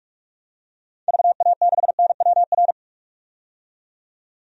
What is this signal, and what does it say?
Morse code, heard as a keyed tone, about 43 wpm, 700 Hz. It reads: VA6NWR
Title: DXCC-100 Call Signs - 6 Characters - 45wpm